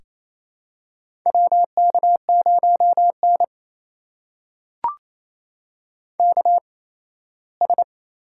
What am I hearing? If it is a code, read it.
Morse code, heard as a keyed tone, about 28 wpm, spaced as for 17 wpm, 700 Hz. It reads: WK0N KS
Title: USA Call Signs - 4 Characters - 8x Character Spacing - 28wpm